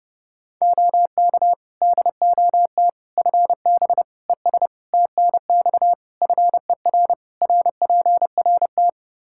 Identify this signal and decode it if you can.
Morse code, heard as a keyed tone, about 30 wpm, 700 Hz. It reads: OK DOT FB ES TNX FER RPRT
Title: Practice QSO - Exchange 2 - Info - 30wpm